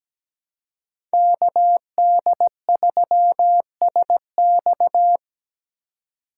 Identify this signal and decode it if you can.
Morse code, heard as a keyed tone, about 17 wpm, 700 Hz. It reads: KD3SX
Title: USA Call Signs - 5 Characters - 6x Character Spacing - 17wpm